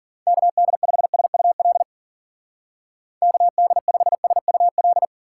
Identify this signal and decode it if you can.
Morse code, heard as a keyed tone, about 49 wpm, 700 Hz. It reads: KD5SUL KD5SUL
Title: Speed Racing - USA 2x3 Call Signs 40wpm